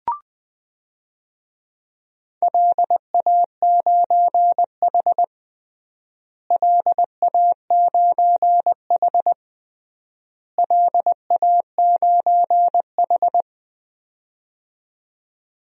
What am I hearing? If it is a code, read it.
Morse code, heard as a keyed tone, about 20 wpm, 700 Hz. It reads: LA9H LA9H LA9H
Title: DXCC-100 Call Signs - 4 Characters - Repeated 3x - 20wpm